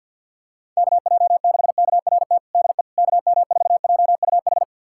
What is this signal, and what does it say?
Morse code, heard as a keyed tone, about 48 wpm, 700 Hz. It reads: KJ6KRT DE KM4JUH
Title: USA Callsign DE Callsign 50wpm